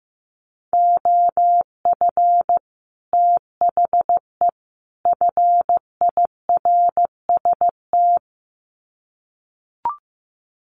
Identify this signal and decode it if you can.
Morse code, heard as a keyed tone, about 15 wpm, 700 Hz. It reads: OF THE FIRST
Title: Sets of 3 Words 15wpm